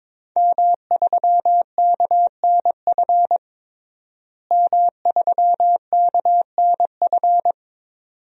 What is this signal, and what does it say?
Morse code, heard as a keyed tone, about 21 wpm, 700 Hz. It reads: M3KNF M3KNF
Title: DXCC-100 Call Signs - 5 Characters - Repeated 2x - 22wpm